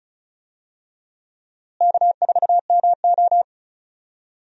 Morse code, heard as a keyed tone, about 36 wpm, 700 Hz. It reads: K4MO
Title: USA Call Signs - 4 Characters - Repeated 3x - 35wpm